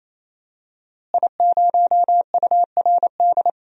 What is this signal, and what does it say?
Morse code, heard as a keyed tone, about 28 wpm, 700 Hz. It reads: I0URD
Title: DXCC-100 Call Signs - 5 Characters - Repeated 3x - 28wpm